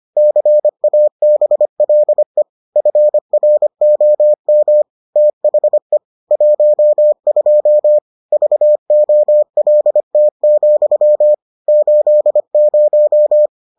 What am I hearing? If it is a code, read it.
Morse code, heard as a keyed tone, about 25 wpm, 600 Hz. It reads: CABLE FROM THE 12 VOLT, 80